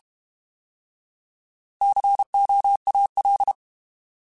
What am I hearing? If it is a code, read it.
Morse code, heard as a keyed tone, about 32 wpm, 800 Hz. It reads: COAL
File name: Alphabetical / Alphabetical mp3s / Cs @30